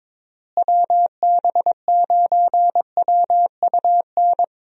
Morse code, heard as a keyed tone, about 21 wpm, 700 Hz. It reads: WB9WUN
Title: USA Call Signs - 6 Characters - Repeated 2x - 22wpm